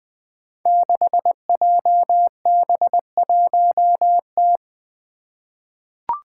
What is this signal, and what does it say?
Morse code, heard as a keyed tone, about 21 wpm, 700 Hz. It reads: 6JB1T
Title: Five Challenging Letter-Number Combinations 20wpm